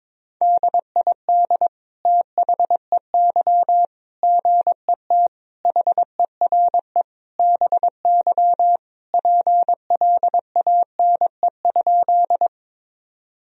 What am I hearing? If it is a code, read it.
Morse code, heard as a keyed tone, about 21 wpm, 700 Hz. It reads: DID THEY GET HERE BY PLANE?
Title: Sentences from Top 1000 Words - Review Part 3 - 22wpm